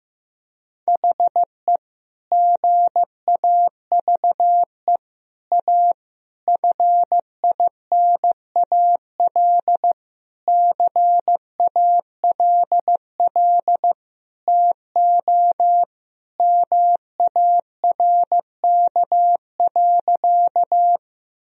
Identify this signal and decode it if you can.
Morse code, heard as a keyed tone, about 16 wpm, 700 Hz. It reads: HE GAVE A FINAL CALL TO MARK.
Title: Sentences from Top 1000 Words - Review Part 2 - 15wpm